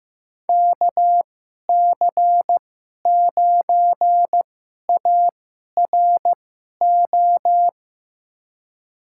Morse code, heard as a keyed tone, about 15 wpm, spaced as for 11 wpm, 700 Hz. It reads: KC9ARO
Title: USA Call Signs - 6 Characters - 2x Character Spacing - 15wpm